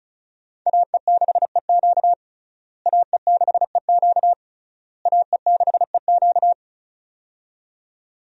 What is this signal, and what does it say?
Morse code, heard as a keyed tone, about 34 wpm, 700 Hz. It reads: AE6EQ AE6EQ AE6EQ
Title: USA Call Signs - 5 Characters - Repeated 3x - 35wpm